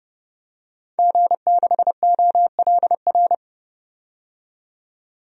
Morse code, heard as a keyed tone, about 30 wpm, 700 Hz. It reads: G6OLR
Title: DXCC-100 Call Signs - 5 Characters - 30wpm